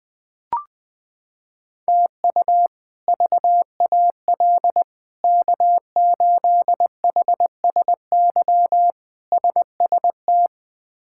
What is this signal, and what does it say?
Morse code, heard as a keyed tone, about 21 wpm, 700 Hz. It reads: TU VAL K8HSY SST
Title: K1USN SST QSO Practice - 20wpm